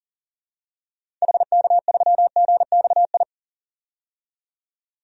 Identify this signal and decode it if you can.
Morse code, heard as a keyed tone, about 40 wpm, 700 Hz. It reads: HK3GXI
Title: DXCC-100 Call Signs - 6 Characters - 40wpm